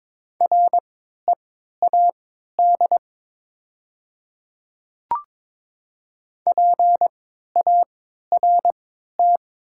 Morse code, heard as a keyed tone, about 22 wpm, spaced as for 12 wpm, 700 Hz. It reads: READ PART
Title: 3x Character Spacing - Top 200 Words - 22wpm